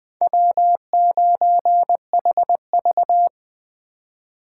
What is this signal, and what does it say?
Morse code, heard as a keyed tone, about 20 wpm, 700 Hz. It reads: W9HV